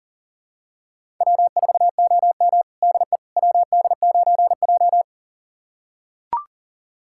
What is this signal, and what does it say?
Morse code, heard as a keyed tone, about 40 wpm, 700 Hz. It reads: W4OM DE WD9J
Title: USA Callsign DE Callsign - 4 Characters - 40wpm